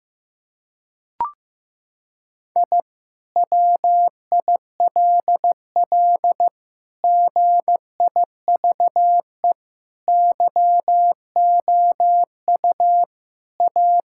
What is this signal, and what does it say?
Morse code, heard as a keyed tone, about 16 wpm, 700 Hz. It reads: I WILL GIVE YOU A
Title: Sentences from Top 1000 Words - Review Part 6 - 15wpm